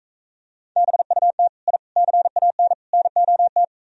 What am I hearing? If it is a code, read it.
Morse code, heard as a keyed tone, about 45 wpm, 700 Hz. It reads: BUT I CAN NOT